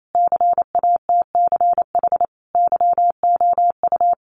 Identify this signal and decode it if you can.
Morse code, heard as a keyed tone, about 28 wpm, 700 Hz. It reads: CATCH YOU
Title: Sentences from Top 1500 Words 28wpm